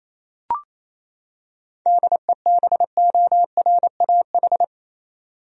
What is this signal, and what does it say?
Morse code, heard as a keyed tone, about 28 wpm, 700 Hz. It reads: DEBORAH
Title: USA Common Names - 28wpm